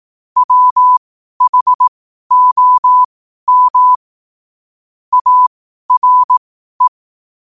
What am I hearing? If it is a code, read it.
Morse code, heard as a keyed tone, about 18 wpm, spaced as for 12 wpm, 1000 Hz. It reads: WHOM ARE
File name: qotdfiles / 12wpm